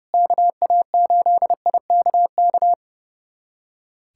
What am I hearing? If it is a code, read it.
Morse code, heard as a keyed tone, about 30 wpm, 700 Hz. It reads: KA8IKK
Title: USA Callsign DE Callsign - 6 Characters - 30wpm